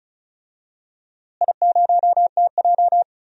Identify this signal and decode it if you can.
Morse code, heard as a keyed tone, about 34 wpm, 700 Hz. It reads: I0TJ
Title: DXCC-100 Call Signs - 4 Characters - Repeated 3x - 35wpm